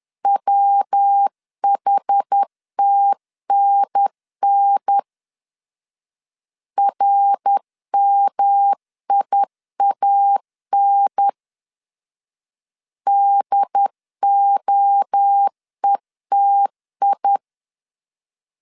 Morse code, heard as a keyed tone, about 10 wpm, 800 Hz. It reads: WHTNN RMIAN DOETI
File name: CW / DET / 1 3 1500C S A 10wpm 3447